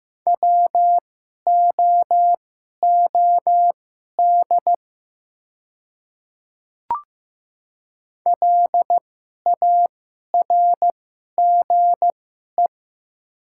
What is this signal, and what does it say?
Morse code, heard as a keyed tone, about 15 wpm, spaced as for 11 wpm, 700 Hz. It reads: WOOD LARGE
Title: 2x Character Spacing - Top 300 Words - 15wpm